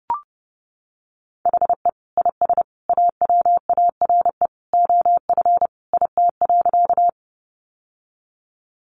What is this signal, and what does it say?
Morse code, heard as a keyed tone, about 30 wpm, 700 Hz. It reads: HE IS AWARE OF IT.
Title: Sentences from Top 1200 Words 30wpm